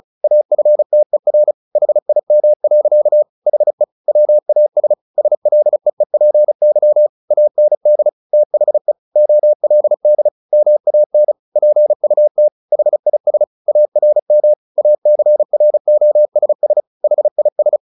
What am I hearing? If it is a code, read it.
Morse code, heard as a keyed tone, about 34 wpm, 600 Hz. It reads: AFTER HIM. HE WAS SLEEPY AND THE OLD MAN PUT HIS ARM ACROSS HIS
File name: the old man and the sea / the old man and the sea 1 35wpm